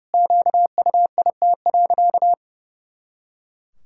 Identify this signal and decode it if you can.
Morse code, heard as a keyed tone, about 31 wpm, 700 Hz. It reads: QUIT.
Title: Pangram Challenge v2 - 30wpm